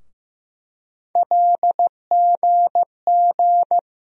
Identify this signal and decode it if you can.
Morse code, heard as a keyed tone, about 15 wpm, 700 Hz. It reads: LGG
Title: USA Call Sign Postfixes - 3 Letters 15wpm